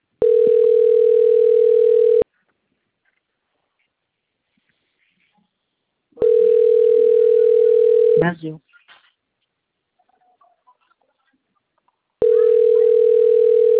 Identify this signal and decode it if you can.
Ringback tone (North America), 440 + 480 Hz, with speech.
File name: out- - - - - .2844